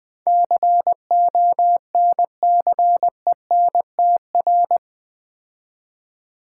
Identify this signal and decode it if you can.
Morse code, heard as a keyed tone, about 20 wpm, 700 Hz. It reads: CONCENTR
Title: Build A Word - 4 Letters At A Time - 11-Letter Words - 20wpm